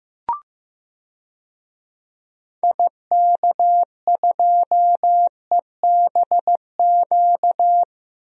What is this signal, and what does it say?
Morse code, heard as a keyed tone, about 15 wpm, 700 Hz. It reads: IK2EBQ
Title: DXCC-100 Call Signs - 6 Characters - Repeated 3x - 15wpm